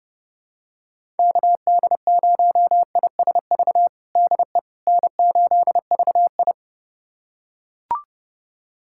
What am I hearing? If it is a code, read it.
Morse code, heard as a keyed tone, about 30 wpm, 700 Hz. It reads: KD0ISV DE N8VI
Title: USA Callsign DE Callsign 30wpm